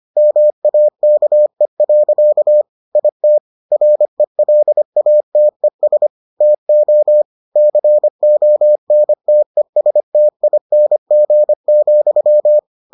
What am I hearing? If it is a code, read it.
Morse code, heard as a keyed tone, about 25 wpm, 600 Hz. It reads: MAKE. IT RELATES TO CONTESTING,